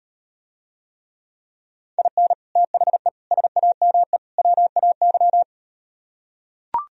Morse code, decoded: IN THE SAME WAY

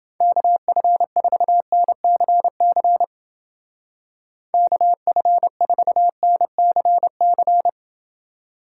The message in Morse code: KF4NCC KF4NCC

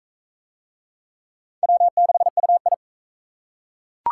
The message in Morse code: W6UI

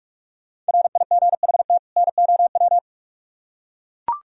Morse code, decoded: RIGHT NOW

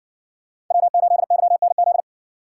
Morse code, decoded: W8YNZ

